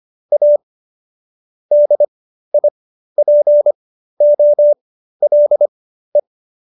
Morse code, heard as a keyed tone, about 25 wpm, spaced as for 13 wpm, 590 Hz. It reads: A DIPOLE